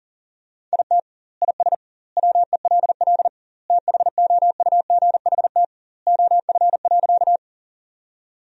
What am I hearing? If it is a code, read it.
Morse code, heard as a keyed tone, about 40 wpm, 700 Hz. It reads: IT IS WELL THOUGHT OF.
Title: 2x Word-Spacing - Sentences from Top 300 Words - 40wpm